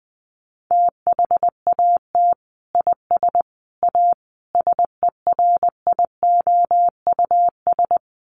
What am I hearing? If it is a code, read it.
Morse code, heard as a keyed tone, about 19 wpm, 700 Hz. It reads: THAT IS A SERIOUS
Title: Sentences from Top 1100 Words 20wpm